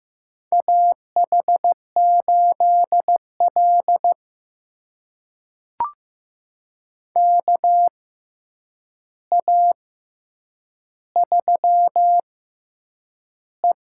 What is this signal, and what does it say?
Morse code, heard as a keyed tone, about 15 wpm, spaced as for 8 wpm, 700 Hz. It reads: AH8L KA3E
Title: USA Call Signs - 4 Characters - 6x Character Spacing - 15wpm